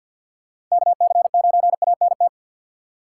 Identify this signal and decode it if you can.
Morse code, heard as a keyed tone, about 49 wpm, 700 Hz. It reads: KC9ANT
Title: USA Call Signs - 6 Characters - 3x Character Spacing - 50wpm